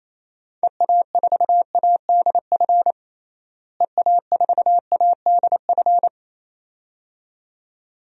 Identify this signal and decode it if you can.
Morse code, heard as a keyed tone, about 27 wpm, 700 Hz. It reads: EA4ADF EA4ADF
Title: DXCC-100 Call Signs - 6 Characters - Repeated 2x - 28wpm